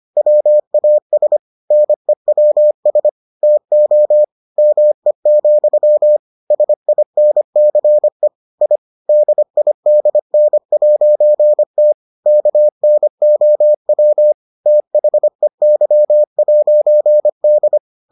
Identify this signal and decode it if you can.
Morse code, heard as a keyed tone, about 25 wpm, 600 Hz. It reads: WAS NEWS TO ME, SINCE I DIDN'T KNOW THEY'D